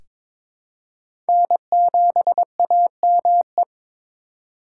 Morse code, heard as a keyed tone, about 22 wpm, 700 Hz. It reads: N7AME
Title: K1USN SST QSO Practice - 22wpm